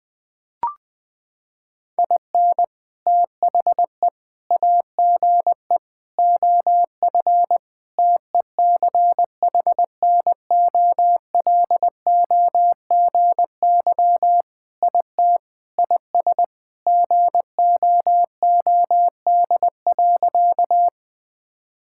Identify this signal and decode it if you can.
Morse code, heard as a keyed tone, about 20 wpm, 700 Hz. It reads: IN THE AGE OF TECHNOLOGY IT IS GOOD.
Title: Sentences with 3-Word Phrases - 20wpm